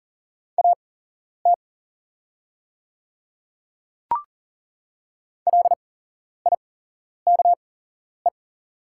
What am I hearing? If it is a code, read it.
Morse code, heard as a keyed tone, about 40 wpm, spaced as for 11 wpm, 700 Hz. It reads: AT LIKE
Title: 8x Character Spacing - Top 100 Words - 40wpm